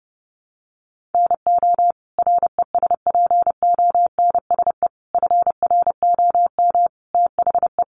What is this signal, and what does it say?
Morse code, heard as a keyed tone, about 30 wpm, 700 Hz. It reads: NO RESPONSE FROM THE
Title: Sentences from Top 1400 Words 30wpm